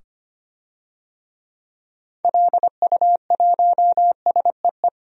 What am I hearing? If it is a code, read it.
Morse code, heard as a keyed tone, about 25 wpm, 690 Hz. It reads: LU1SEE
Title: DXCC-100 Call Signs - 6 Characters - 25wpm